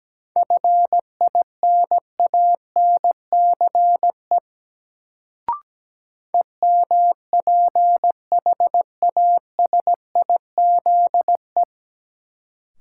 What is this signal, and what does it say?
Morse code, heard as a keyed tone, about 17 wpm, 700 Hz. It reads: FINANCE EMPHASIZE
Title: Sentences from Top 1500 Words - New Word Drill - 17wpm